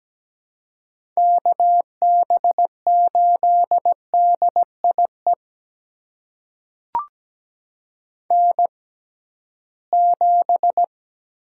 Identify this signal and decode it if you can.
Morse code, heard as a keyed tone, about 17 wpm, spaced as for 13 wpm, 700 Hz. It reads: KB8DIE N7
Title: USA Call Signs - 6x Character Spacing - 17wpm